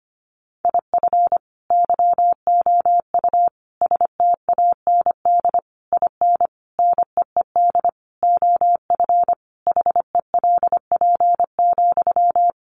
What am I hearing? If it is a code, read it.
Morse code, heard as a keyed tone, about 25 wpm, 700 Hz. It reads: IF YOU STAND IN NEED OF HELP,
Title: Sentences from Top 1200 Words 25wpm